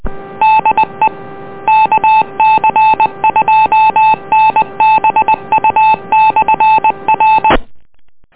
Morse code, heard as a keyed tone, about 20 wpm, 870 Hz. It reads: DE KC2NBU/R